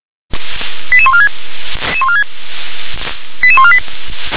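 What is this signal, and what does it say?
Five-tone selective call (ZVEI), heard as quick runs of five tones, tones of about 70 ms. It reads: 88146 88148
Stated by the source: ZVEI five-tone selective call received by SDR.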